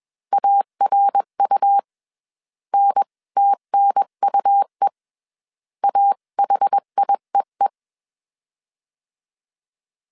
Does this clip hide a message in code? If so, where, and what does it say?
Yes: Morse code from 0.3 to 7.7 s, heard as a keyed tone, about 21 wpm, 800 Hz. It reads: ARU NTNUE AHIEE